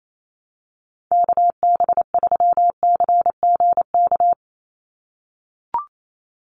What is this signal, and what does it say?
Morse code, heard as a keyed tone, about 28 wpm, 700 Hz. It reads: KB3CGK
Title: USA Call Signs 28wpm